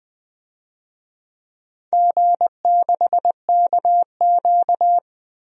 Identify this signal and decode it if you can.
Morse code, heard as a keyed tone, about 20 wpm, 700 Hz. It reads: G6KQ
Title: DXCC-100 Call Signs - 4 Characters - 20wpm